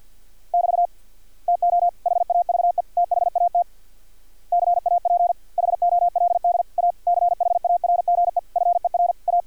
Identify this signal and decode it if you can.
Morse code, heard as a keyed tone, about 50 wpm, 700 Hz. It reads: <BT> TO HAVE THAT CAP HOLD A CHARGE FER A
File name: archived / CRSnet